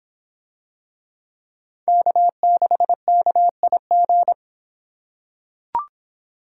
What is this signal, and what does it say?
Morse code, heard as a keyed tone, about 26 wpm, 700 Hz. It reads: K6KIG